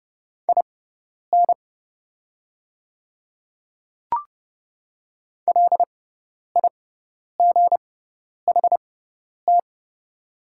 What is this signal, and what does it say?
Morse code, heard as a keyed tone, about 30 wpm, spaced as for 10 wpm, 700 Hz. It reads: IN LIGHT
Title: 6x Character Spacing - Top 200 Words - 30wpm